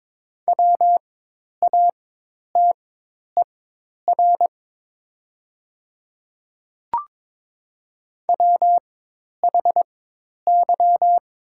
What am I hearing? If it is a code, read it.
Morse code, heard as a keyed tone, about 21 wpm, spaced as for 10 wpm, 700 Hz. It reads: WATER WHY
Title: 4x Character Spacing - Top 200 Words - 22wpm